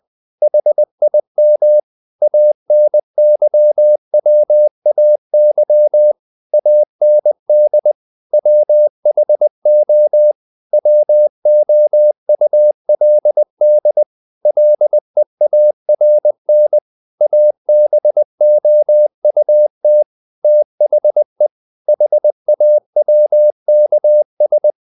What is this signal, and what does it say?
Morse code, heard as a keyed tone, about 20 wpm, 600 Hz. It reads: HIM ANYWAY AND WHO WOULD LEARN ABOUT THE HAWKS
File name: the old man and the sea / the old man and the sea part 1 20wpm